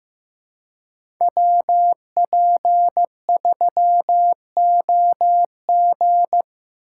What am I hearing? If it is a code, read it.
Morse code, heard as a keyed tone, about 15 wpm, 700 Hz. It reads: WP3OG
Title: USA Call Signs - 5 Characters - 4x Character Spacing - 15wpm